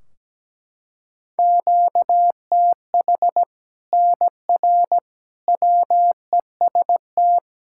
Morse code, heard as a keyed tone, about 17 wpm, 700 Hz. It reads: QTH NR WEST
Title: Practice QSO - Exchange 1 - Beginning - 17wpm